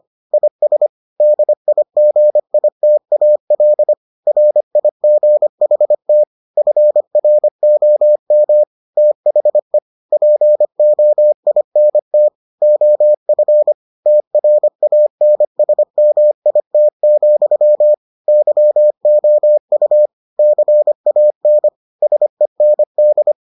Morse code, heard as a keyed tone, about 25 wpm, 600 Hz. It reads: IS DIGITAL RIGHT FROM THE POINT OF TRANSMIT, YOU CAN SEND